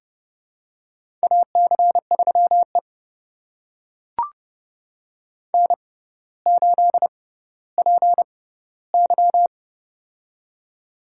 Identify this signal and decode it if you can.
Morse code, heard as a keyed tone, about 31 wpm, spaced as for 16 wpm, 700 Hz. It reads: AC3E N8PY
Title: USA Call Signs - 4 Characters - 6x Character Spacing - 30wpm